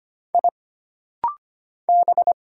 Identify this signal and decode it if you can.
Morse code, heard as a keyed tone, about 25 wpm, 700 Hz. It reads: I B
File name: Single Character (Letter:Number:Punctuation:Prosign) - Rapid-Fire - 25wpm